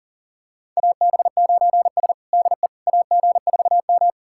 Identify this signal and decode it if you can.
Morse code, heard as a keyed tone, about 37 wpm, 700 Hz. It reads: AB9S DE AG4M
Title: USA Callsign DE Callsign - 4 Characters - 40wpm